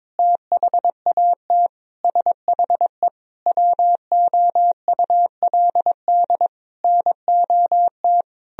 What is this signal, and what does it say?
Morse code, heard as a keyed tone, about 22 wpm, 700 Hz. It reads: THAT SHE WOULD NOT
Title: Sentences from Top 1000 Words - Review Part 2 - 22wpm